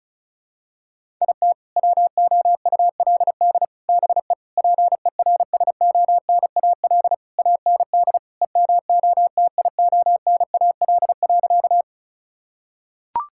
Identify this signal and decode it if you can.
Morse code, heard as a keyed tone, about 36 wpm, 700 Hz. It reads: IT WOULD BE PERSONAL AND EMOTIONAL.
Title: Sentences from Top 1000 Words - Review Part 5 - 35wpm